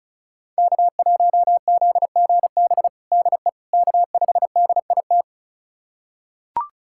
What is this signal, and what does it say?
Morse code, heard as a keyed tone, about 36 wpm, 700 Hz. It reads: K1ZGB DE K5DIT
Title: USA Callsign DE Callsign - 5 Characters - 35wpm